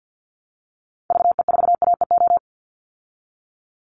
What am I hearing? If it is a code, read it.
Morse code, heard as a keyed tone, about 50 wpm, 700 Hz. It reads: VE4AEO
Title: Call Signs Repeated 2x - 50wpm